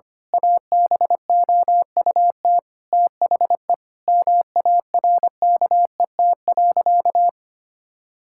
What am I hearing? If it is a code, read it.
Morse code, heard as a keyed tone, about 25 wpm, 700 Hz. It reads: ABOUT THE MARKET.
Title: Sentences from Top 1000 Words - Review Part 5 - 25wpm